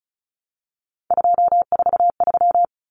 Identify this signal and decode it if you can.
Morse code, heard as a keyed tone, about 34 wpm, 700 Hz. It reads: 243
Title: Common Cut Numbers - 3-Digit 35wpm